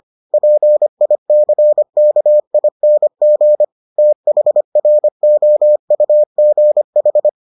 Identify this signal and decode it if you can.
Morse code, heard as a keyed tone, about 25 wpm, 600 Hz. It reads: PICKING THROUGH